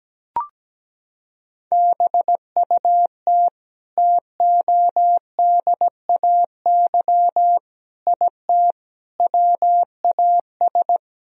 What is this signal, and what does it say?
Morse code, heard as a keyed tone, about 17 wpm, 700 Hz. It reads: BUT TODAY IT WAS
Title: Sentences from Top 1000 Words - Review Part 4 - 17wpm